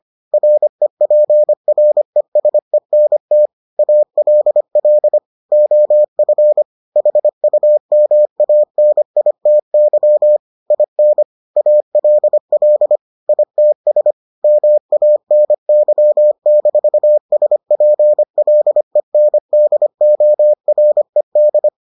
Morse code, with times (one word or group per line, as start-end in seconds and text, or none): 0.3-3.5 REPRESENT
3.8-5.2 ALL
5.5-6.6 OF
7.0-10.4 HUMANITY
10.7-11.2 IN
11.6-13.0 ALL
13.3-14.1 ITS
14.4-21.7 MANY-SPLENDORED